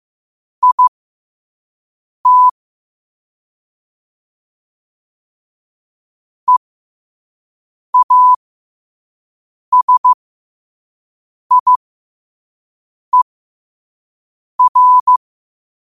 Morse code, heard as a keyed tone, about 15 wpm, spaced as for 5 wpm, 1000 Hz. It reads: IT EASIER